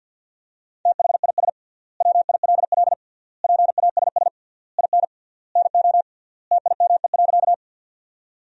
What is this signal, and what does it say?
Morse code, heard as a keyed tone, about 50 wpm, 700 Hz. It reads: THIS WILL PASS IN NO TIME.
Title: 3x Word-Spacing - Sentences from Top 400 Words - 50wpm